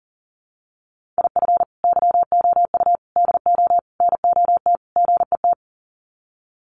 Morse code, heard as a keyed tone, about 40 wpm, 700 Hz. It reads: IF YOU DO NOT GET
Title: Sets of 5 Words Encore2 40wpm